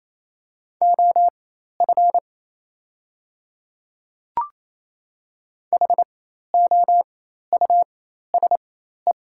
Morse code, heard as a keyed tone, about 27 wpm, spaced as for 13 wpm, 690 Hz. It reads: OF HOUSE
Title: 4x Character Spacing - Top 200 Words - 28wpm